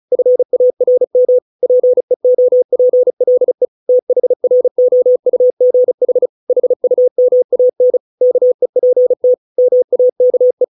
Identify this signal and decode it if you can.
Morse code, heard as a keyed tone, about 34 wpm, 500 Hz. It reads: FARM PEOPLE THROUGH HUMAN KEPT MAKE